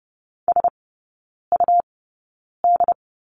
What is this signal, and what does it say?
Morse code, heard as a keyed tone, about 30 wpm, spaced as for 9 wpm, 700 Hz. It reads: SUD
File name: Lesson 17e - Letter u - Three Characters 30wpm